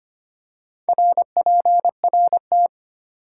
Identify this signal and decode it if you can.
Morse code, heard as a keyed tone, about 25 wpm, 700 Hz. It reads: RPRT
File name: Top 90 QSO Elements - 4x Character Spacing - 25wpm